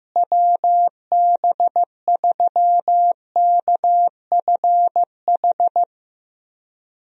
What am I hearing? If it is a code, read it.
Morse code, heard as a keyed tone, about 15 wpm, 700 Hz. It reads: WB3KFH
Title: USA Call Signs - 6 Characters - Repeated 2x - 15wpm